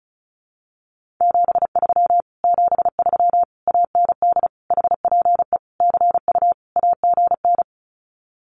Morse code, heard as a keyed tone, about 36 wpm, 700 Hz. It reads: 73 73 AND HPE CU AGN
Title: Common QSO Phrases Encore 35wpm